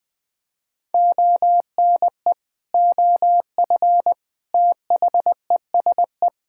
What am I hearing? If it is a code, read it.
Morse code, heard as a keyed tone, about 20 wpm, 700 Hz. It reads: ONE OF THESE
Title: Sentences with 8-Word Phrases - Spoken Beginning - 20wpm